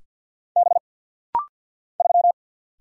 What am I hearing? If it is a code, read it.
Morse code, heard as a keyed tone, about 50 wpm, 700 Hz. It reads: B 3
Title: Single Letter-Number - Rapid-Fire 50wpm